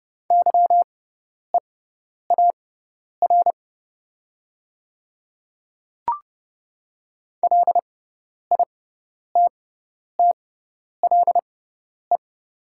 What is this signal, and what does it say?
Morse code, heard as a keyed tone, about 30 wpm, spaced as for 10 wpm, 700 Hz. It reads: YEAR LITTLE
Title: 6x Character Spacing - Top 300 Words - 30wpm